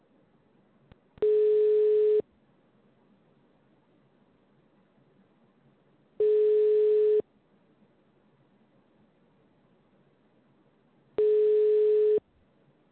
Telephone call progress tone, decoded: ringback tone